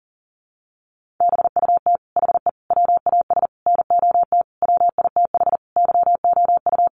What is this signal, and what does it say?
Morse code, heard as a keyed tone, about 40 wpm, 700 Hz. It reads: BUT HE WAS NOT WITH YOU
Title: Sentences from Top 1500 Words 40wpm